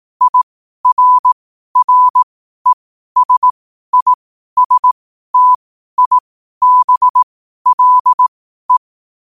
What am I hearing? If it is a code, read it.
Morse code, heard as a keyed tone, about 18 wpm, spaced as for 12 wpm, 1000 Hz. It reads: IRRESISTIBLE
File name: qotdfiles / 12wpm